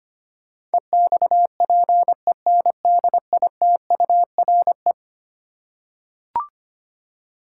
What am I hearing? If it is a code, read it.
Morse code, heard as a keyed tone, about 25 wpm, 700 Hz. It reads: EXPENDITURE
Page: morse-code-ninja-practice-25wpm